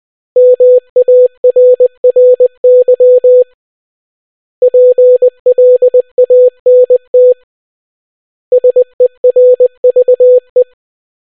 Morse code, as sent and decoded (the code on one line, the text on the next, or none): -- .- .-. .-. -.-- / .--. .-.. .- -. - / ... .-. ...- .
MARRY PLANT SERVE